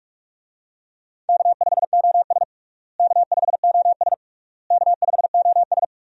Morse code, heard as a keyed tone, about 44 wpm, 700 Hz. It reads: K5OS K5OS K5OS
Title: USA Call Signs - 4 Characters - Repeated 3x - 45wpm